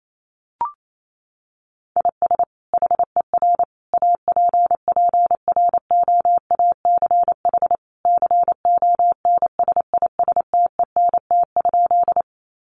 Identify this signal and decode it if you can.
Morse code, heard as a keyed tone, about 28 wpm, 700 Hz. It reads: IS HER APPROACH CONSISTENT?
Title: Sentences from Top 1500 Words 28wpm